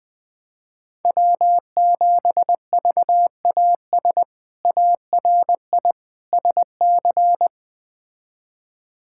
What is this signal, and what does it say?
Morse code, heard as a keyed tone, about 20 wpm, 700 Hz. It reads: W7VAS ARI SC
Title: K1USN SST QSO Practice - 20wpm